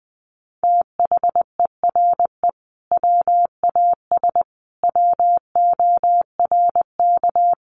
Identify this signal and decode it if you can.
Morse code, heard as a keyed tone, about 20 wpm, 700 Hz. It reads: THERE WAS WORK
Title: Sentences from Top 200 Words 20wpm